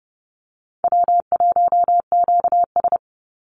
Morse code, heard as a keyed tone, about 30 wpm, 700 Hz. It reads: W1QS
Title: Call Signs Repeated 3x - 30wpm